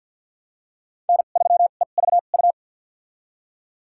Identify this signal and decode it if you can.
Morse code, heard as a keyed tone, about 53 wpm, spaced as for 37 wpm, 700 Hz. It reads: N3EVU
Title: USA Call Signs - 5 Characters - 2x Character Spacing - 50wpm